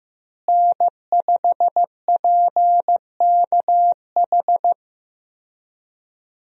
Morse code, heard as a keyed tone, about 15 wpm, 700 Hz. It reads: N5PKH
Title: USA Call Signs - 5 Characters - Repeated 2x - 15wpm